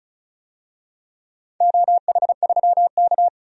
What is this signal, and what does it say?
Morse code, heard as a keyed tone, about 36 wpm, 700 Hz. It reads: OH3K
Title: DXCC-100 Call Signs - 4 Characters - 35wpm